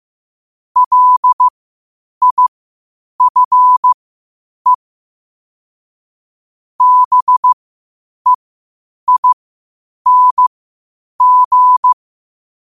Morse code, heard as a keyed tone, about 15 wpm, spaced as for 8 wpm, 1000 Hz. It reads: LIFE BEING
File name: qotdfiles / 8wpm